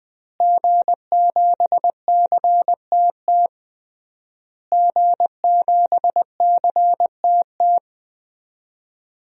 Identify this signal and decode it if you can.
Morse code, heard as a keyed tone, about 20 wpm, 700 Hz. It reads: G7CTT G7CTT
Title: DXCC-100 Call Signs - 5 Characters - Repeated 3x - 20wpm